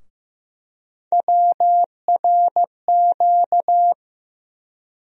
Morse code, heard as a keyed tone, about 15 wpm, 700 Hz. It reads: WRQ
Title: USA Call Sign Postfixes - 3 Letters 15wpm